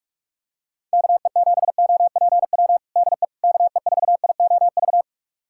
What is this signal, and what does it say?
Morse code, heard as a keyed tone, about 46 wpm, 700 Hz. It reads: KE7OPW DE KE4IOV
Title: USA Callsign DE Callsign - 6 Characters - 45wpm